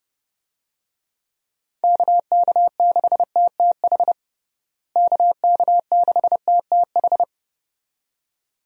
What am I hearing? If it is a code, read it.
Morse code, heard as a keyed tone, about 32 wpm, 700 Hz. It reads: KK6TTH KK6TTH
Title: USA Call Signs - 6 Characters - Repeated 2x - 30wpm